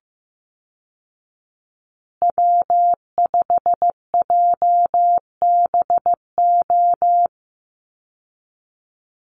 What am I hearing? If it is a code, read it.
Morse code, heard as a keyed tone, about 15 wpm, 700 Hz. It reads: W5JBO
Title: Call Signs Repeated 2x - 15wpm